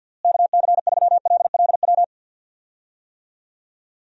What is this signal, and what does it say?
Morse code, heard as a keyed tone, about 49 wpm, 700 Hz. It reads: KC3LLW
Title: USA Call Signs - 6 Characters - Repeated 3x - 50wpm